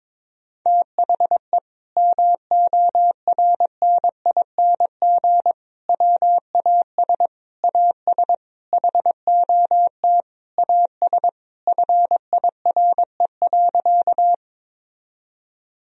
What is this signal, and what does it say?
Morse code, heard as a keyed tone, about 22 wpm, 700 Hz. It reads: THE MORNING WAS AS HOT AS FIRE.
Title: Sentences from Top 1000 Words - Review Part 3 - 22wpm